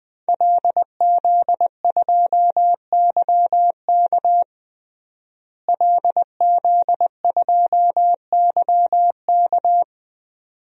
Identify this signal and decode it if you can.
Morse code, heard as a keyed tone, about 21 wpm, 700 Hz. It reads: LZ2YK LZ2YK
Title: DXCC-100 Call Signs - 5 Characters - Repeated 2x - 20wpm